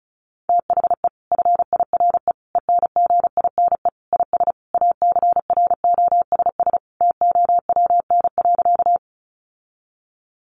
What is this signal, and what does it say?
Morse code, heard as a keyed tone, about 34 wpm, 700 Hz. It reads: THE FIRE ENGINE IS ACROSS TOWN.
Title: Sentences from Top 1500 Words 35wpm